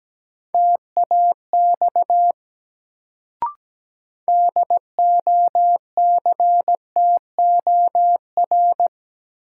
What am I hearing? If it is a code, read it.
Morse code, heard as a keyed tone, about 17 wpm, 700 Hz. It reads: TAX DOCTOR
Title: Sentences from Top 1500 Words - New Word Drill Alt - 17wpm